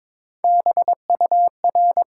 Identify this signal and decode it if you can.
Morse code, heard as a keyed tone, about 21 wpm, 700 Hz. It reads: BUR
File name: Top 90 QSO Elements - 6x Character Spacing - 22wpm